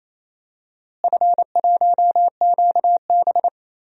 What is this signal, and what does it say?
Morse code, heard as a keyed tone, about 28 wpm, 700 Hz. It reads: F1QB
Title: DXCC-100 Call Signs - 4 Characters - Repeated 2x - 28wpm